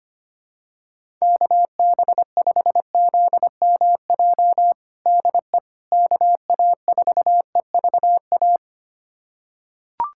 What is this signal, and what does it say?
Morse code, heard as a keyed tone, about 25 wpm, 700 Hz. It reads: KB5ZMJ DE KA4EVA
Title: USA Callsign DE Callsign - 6 Characters - 25wpm